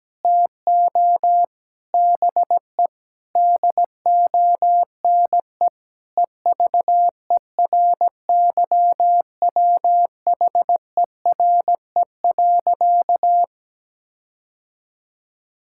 Morse code, heard as a keyed tone, about 17 wpm, 700 Hz. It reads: TO BE DONE EVERYWHERE.
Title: Sentences from Top 1600 Words 17wpm